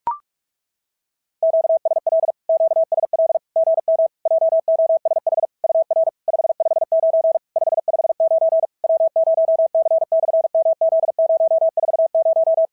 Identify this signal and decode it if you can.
Morse code, heard as a keyed tone, about 44 wpm, 650 Hz. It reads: QSL QSL GM JOSH UR 559 559 W0C/MZ040
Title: SOTA QSO Practice 45wpm